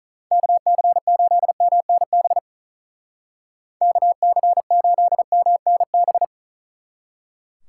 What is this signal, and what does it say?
Morse code, heard as a keyed tone, about 43 wpm, 700 Hz. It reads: KC8MNB KC8MNB
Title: Speed Racing - USA 2x3 Call Signs 35wpm